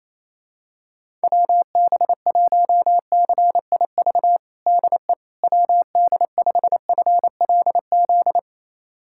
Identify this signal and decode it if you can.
Morse code, heard as a keyed tone, about 28 wpm, 700 Hz. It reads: WB1CIV DE WD5FLZ
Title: USA Callsign DE Callsign - 6 Characters - 28wpm